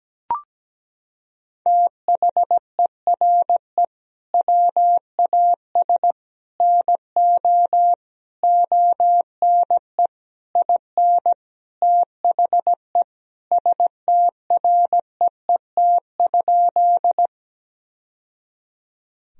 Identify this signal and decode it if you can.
Morse code, heard as a keyed tone, about 17 wpm, 700 Hz. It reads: THERE WAS NO ONE IN THE STREET?
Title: Sentences from Top 1000 Words - Review Part 3 - 17wpm